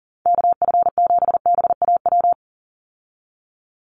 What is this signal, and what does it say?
Morse code, heard as a keyed tone, about 37 wpm, 700 Hz. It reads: KF7BAW
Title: USA Call Signs 40wpm